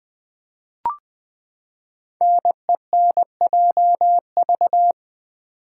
Morse code, heard as a keyed tone, about 20 wpm, 700 Hz. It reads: NENJV
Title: Code Groups - 5 Letters - 20wpm